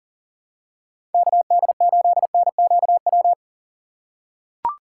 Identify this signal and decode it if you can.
Morse code, heard as a keyed tone, about 40 wpm, 700 Hz. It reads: KD8NQW
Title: USA Call Signs - 6 Characters - 3x Character Spacing - 40wpm